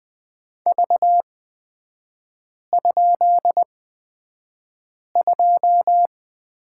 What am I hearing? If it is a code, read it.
Morse code, heard as a keyed tone, about 20 wpm, spaced as for 5 wpm, 690 Hz. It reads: V?2
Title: 236 Lesson 38e - Letter q - Three Characters 20wpm